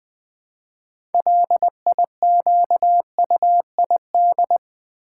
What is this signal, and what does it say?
Morse code, heard as a keyed tone, about 20 wpm, 690 Hz. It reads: LIQUID
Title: Build A Word - 2 Letters At A Time - 6-Letter Words - 20wpm